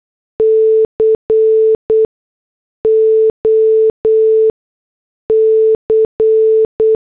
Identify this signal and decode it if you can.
Morse code, heard as a keyed tone, about 8 wpm, 440 Hz. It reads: C O C
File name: morse / headlines